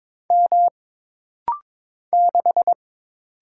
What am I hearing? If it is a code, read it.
Morse code, heard as a keyed tone, about 22 wpm, 700 Hz. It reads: M 6
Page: This rapid-fire timing is geared to push your ICR (Instant Character Recognition) skills.